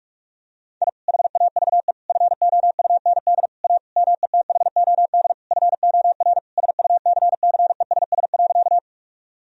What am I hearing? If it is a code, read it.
Morse code, heard as a keyed tone, about 41 wpm, 700 Hz. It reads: I HAVE FOUND A METHOD FOR SUCCESS.
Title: Sentences from Top 1000 Words - Review Part 3 - 45wpm